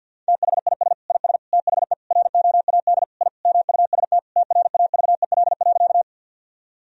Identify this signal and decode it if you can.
Morse code, heard as a keyed tone, about 50 wpm, 700 Hz. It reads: THIS IS THE ROAD I MUST TRAVEL.
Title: Sentences from Top 1000 Words - Review Part 2 - 50wpm